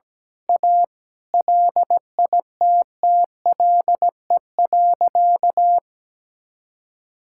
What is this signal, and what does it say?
Morse code, heard as a keyed tone, about 17 wpm, 700 Hz. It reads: A LITTLE.
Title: Sentences from Top 1000 Words - Review Part 5 - 17wpm